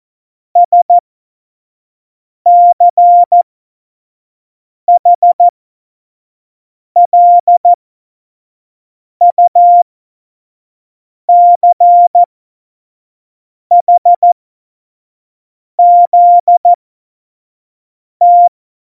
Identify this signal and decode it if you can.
Morse code, heard as a keyed tone, about 14 wpm, spaced as for 5 wpm, 700 Hz. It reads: SCHLUCHZT
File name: Morsekurs / Witze / Witz 03.Juli 2018, 25 F70